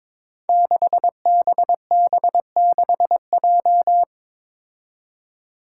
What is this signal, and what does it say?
Morse code, heard as a keyed tone, about 22 wpm, 700 Hz. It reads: 6BB6J
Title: Five Challenging Letter-Number Combinations 22wpm